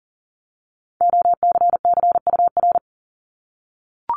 Morse code, heard as a keyed tone, about 39 wpm, 700 Hz. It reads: OCCUR
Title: Multi-letter Phonemes 40wpm